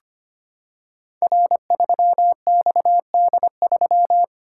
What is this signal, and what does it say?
Morse code, heard as a keyed tone, about 25 wpm, 700 Hz. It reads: R3XD3
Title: Code Groups - 5 Letter:Numbers - 25wpm